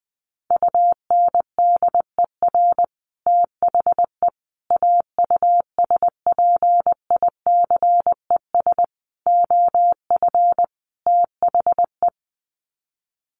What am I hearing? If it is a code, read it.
Morse code, heard as a keyed tone, about 20 wpm, 700 Hz. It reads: UNDER THE AUSPICES OF THE
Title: Sets of 5 Words Encore2 20wpm